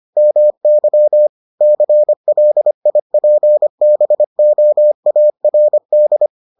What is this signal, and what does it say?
Morse code, heard as a keyed tone, about 25 wpm, 600 Hz. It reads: MY CLIPBOARD